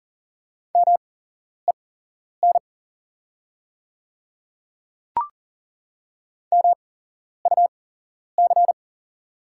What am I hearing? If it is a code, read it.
Morse code, heard as a keyed tone, about 40 wpm, spaced as for 11 wpm, 690 Hz. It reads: MEN MUC
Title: 8x Character Spacing - Top 300 Words - 40wpm